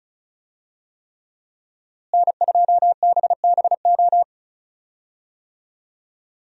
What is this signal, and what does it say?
Morse code, heard as a keyed tone, about 36 wpm, 700 Hz. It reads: N2BBO
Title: USA Call Signs - 5 Characters - Repeated 3x - 35wpm